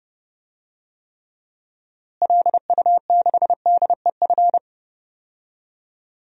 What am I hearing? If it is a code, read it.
Morse code, heard as a keyed tone, about 30 wpm, 700 Hz. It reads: LU6DEF
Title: DXCC-100 Call Signs - 6 Characters - Repeated 2x - 30wpm